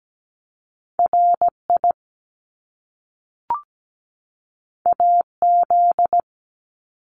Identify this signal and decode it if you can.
Morse code, heard as a keyed tone, about 17 wpm, 700 Hz. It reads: RI AZ
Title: ARRL Sections 17wpm